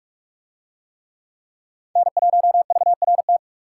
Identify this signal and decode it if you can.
Morse code, heard as a keyed tone, about 45 wpm, 700 Hz. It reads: N1VRT